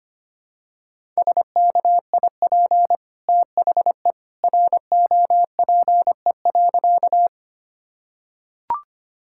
Morse code, heard as a keyed tone, about 25 wpm, 700 Hz. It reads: SKIP THE ROPE.
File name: Easy Sentences with Hints - 25wpm